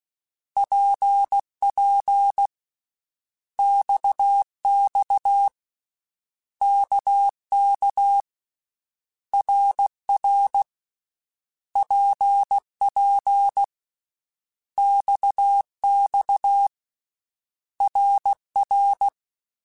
Morse code, heard as a keyed tone, about 16 wpm, 800 Hz. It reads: PP XX KK RR PP XX RR
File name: Basic Letter Groups By7Groups / RKPX / RKPX@13